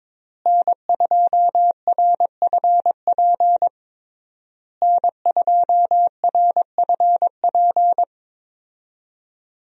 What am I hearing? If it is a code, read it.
Morse code, heard as a keyed tone, about 21 wpm, 700 Hz. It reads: N2RFP N2RFP
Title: USA Call Signs - 5 Characters - Repeated 3x - 22wpm